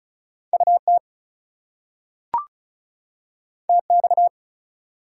Morse code, heard as a keyed tone, about 36 wpm, 700 Hz. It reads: UT TX
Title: US States and Canadian Provinces 35wpm